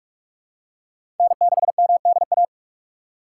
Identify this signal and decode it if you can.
Morse code, heard as a keyed tone, about 46 wpm, 700 Hz. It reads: N6MDA
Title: USA Call Signs - 6x Character Spacing - 45wpm